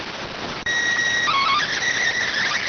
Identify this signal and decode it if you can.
SSTV (slow-scan television), PD 290.